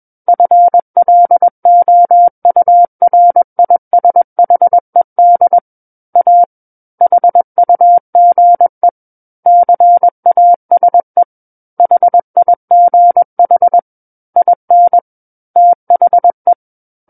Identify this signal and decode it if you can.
Morse code, heard as a keyed tone, about 21 wpm, 700 Hz. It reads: FLOURISHED A HUGE CASE HIGH IN THE
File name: Book One / 12-Chapter 12 - 21 WPM